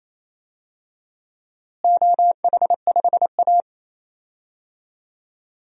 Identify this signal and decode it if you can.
Morse code, heard as a keyed tone, about 27 wpm, 700 Hz. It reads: OH5A
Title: DXCC-100 Call Signs - 4 Characters - Repeated 3x - 28wpm